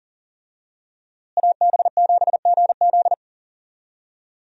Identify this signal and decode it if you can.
Morse code, heard as a keyed tone, about 40 wpm, 700 Hz. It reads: AB7GZ
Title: USA Call Signs - 5 Characters - Repeated 2x - 40wpm